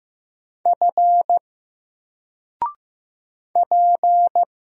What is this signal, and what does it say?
Morse code, heard as a keyed tone, about 15 wpm, 700 Hz. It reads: F P
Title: Single Character (Letter:Number:Punctuation:Prosign) 15wpm